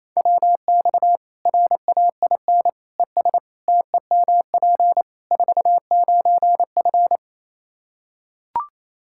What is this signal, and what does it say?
Morse code, heard as a keyed tone, about 28 wpm, 700 Hz. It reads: WX RAIN ES TEMP 49F
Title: Practice QSO - Exchange 2 - Info - 28wpm